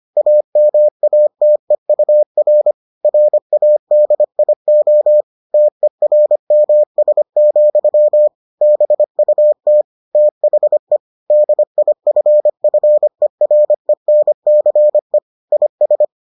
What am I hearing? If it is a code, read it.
Morse code, heard as a keyed tone, about 25 wpm, 600 Hz. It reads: AMATEUR RADIO TERMS, BUT THE DIFFERENCE IS